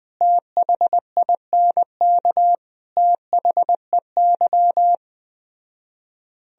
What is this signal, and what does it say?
Morse code, heard as a keyed tone, about 20 wpm, 700 Hz. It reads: THINK THEY
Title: Speed-Racing - Sets of 5 Words 17wpm